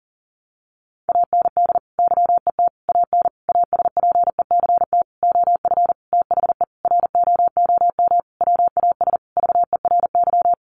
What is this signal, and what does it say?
Morse code, heard as a keyed tone, about 40 wpm, 700 Hz. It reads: AND YET AN ASPECT OF THE ROOM WAS VERY
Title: Sentences from Top 1300 Words 40wpm